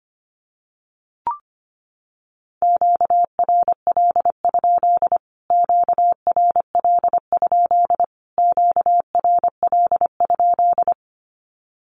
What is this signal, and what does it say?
Morse code, heard as a keyed tone, about 25 wpm, 700 Hz. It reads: QRL? QRL? QRL?
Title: Q-Codes Repeated 3x 25wpm